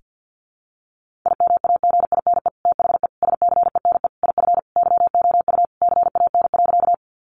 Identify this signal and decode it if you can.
Morse code, heard as a keyed tone, about 50 wpm, 700 Hz. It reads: IMAGINE THE SCENE IF YOU CAN.
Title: Sentences from Top 1400 Words 50wpm